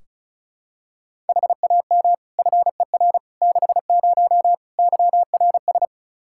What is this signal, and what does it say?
Morse code, heard as a keyed tone, about 34 wpm, 700 Hz. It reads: HAM FER 60 YRS
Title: K1USN SST QSO Practice 35wpm